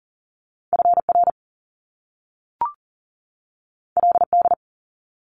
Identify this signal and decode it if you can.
Morse code, heard as a keyed tone, about 40 wpm, 700 Hz. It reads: FR LD
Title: Top Two Letter Combinations in Words 40wpm